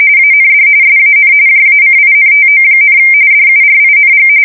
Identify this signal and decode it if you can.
RTTY, about 45.45 baud. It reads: AND SAN JACINTO, DURING
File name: fsk170 / USS-Missouri